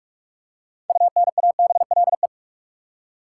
Morse code, heard as a keyed tone, about 41 wpm, 700 Hz. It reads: UNABLE
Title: Six Letter Words - 45wpm